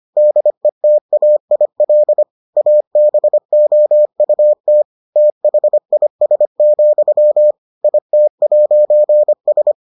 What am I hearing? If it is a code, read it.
Morse code, heard as a keyed tone, about 25 wpm, 600 Hz. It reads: DETAIL ABOUT THIS, IT'S